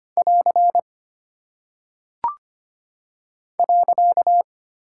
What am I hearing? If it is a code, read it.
Morse code, heard as a keyed tone, about 25 wpm, 700 Hz. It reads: <AR> .
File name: practice / 25wpm / Single Punctuation-Prosign - 25wpm